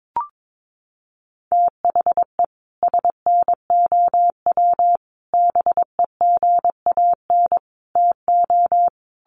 Morse code, heard as a keyed tone, about 22 wpm, 700 Hz. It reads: THE SNOW BEGAN TO
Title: Sentences from Top 1500 Words 22wpm